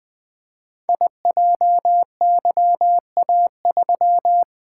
Morse code, heard as a keyed tone, about 20 wpm, 700 Hz. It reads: IJYA3
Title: Code Groups - 5 Letter:Numbers - 20wpm